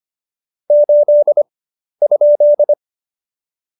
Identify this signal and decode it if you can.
Morse code, heard as a keyed tone, about 25 wpm, spaced as for 11 wpm, 590 Hz. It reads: 8?
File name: Jokes-25-char-10-eff